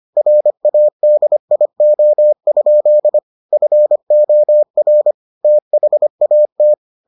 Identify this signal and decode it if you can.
Morse code, heard as a keyed tone, about 25 wpm, 600 Hz. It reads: RADIO? FOR THAT